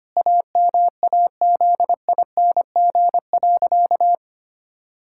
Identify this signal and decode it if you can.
Morse code, heard as a keyed tone, about 25 wpm, 700 Hz. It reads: AMAZING.
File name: practice / 25wpm / Sentences with 3-Word Phrases - 25wpm